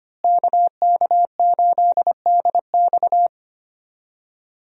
Morse code, heard as a keyed tone, about 25 wpm, 700 Hz. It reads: KK8DX
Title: USA Call Signs - 5 Characters - Repeated 2x - 25wpm